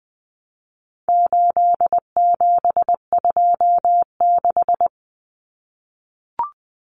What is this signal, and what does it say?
Morse code, heard as a keyed tone, about 20 wpm, 700 Hz. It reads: 8726
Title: Common Cut Numbers - 4-Digit 20wpm